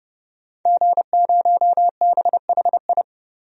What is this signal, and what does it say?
Morse code, heard as a keyed tone, about 30 wpm, 700 Hz. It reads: G0BHI
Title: DXCC-100 Call Signs - 5 Characters - 30wpm